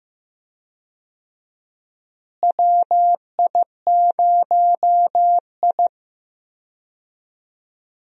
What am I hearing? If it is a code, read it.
Morse code, heard as a keyed tone, about 15 wpm, 700 Hz. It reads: WI0I
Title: USA Call Signs - 4 Characters - Repeated 2x - 15wpm